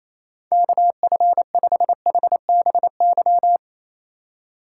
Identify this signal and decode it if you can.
Morse code, heard as a keyed tone, about 33 wpm, 700 Hz. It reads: KF5HBY
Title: Speed Racing - USA Call Signs 28wpm